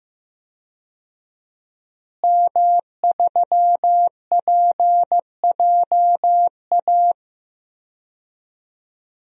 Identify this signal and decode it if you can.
Morse code, heard as a keyed tone, about 15 wpm, 700 Hz. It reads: M3PJA
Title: DXCC-100 Call Signs - 5 Characters - Repeated 2x - 15wpm